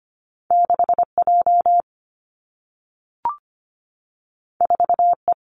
Morse code, heard as a keyed tone, about 25 wpm, 700 Hz. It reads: 6J 4E
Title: Two Challenging Letter-Number Combinations 25wpm